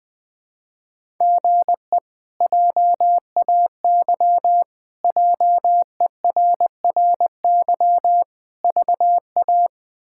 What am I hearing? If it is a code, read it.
Morse code, heard as a keyed tone, about 21 wpm, 700 Hz. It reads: GE JAY JERRY VA
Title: K1USN SST QSO Practice - 20wpm